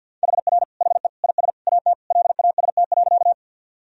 Morse code, heard as a keyed tone, about 50 wpm, 700 Hz. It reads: HR HE IS AT LAST.
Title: Sentences from Top 300 Words and Abbreviations 50wpm